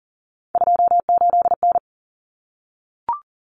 Morse code, heard as a keyed tone, about 40 wpm, 700 Hz. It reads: 28N
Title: Common Cut Numbers - 3-Digit 40wpm